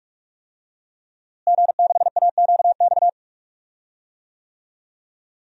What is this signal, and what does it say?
Morse code, heard as a keyed tone, about 46 wpm, 700 Hz. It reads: G6AQX
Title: DXCC-100 Call Signs - 5 Characters - Repeated 3x - 45wpm